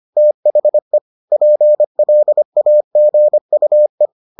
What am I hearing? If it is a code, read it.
Morse code, heard as a keyed tone, about 25 wpm, 600 Hz. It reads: THE PLAGUE